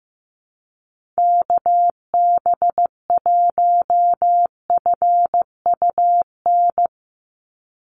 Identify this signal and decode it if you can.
Morse code, heard as a keyed tone, about 15 wpm, 700 Hz. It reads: KB1FUN
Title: USA Call Signs 15wpm